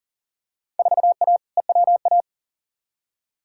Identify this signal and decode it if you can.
Morse code, heard as a keyed tone, about 40 wpm, 700 Hz. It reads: 4A EWA
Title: Field Day Exchanges - 40wpm